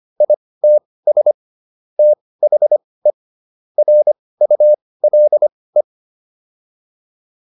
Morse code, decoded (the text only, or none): ITS THE RULE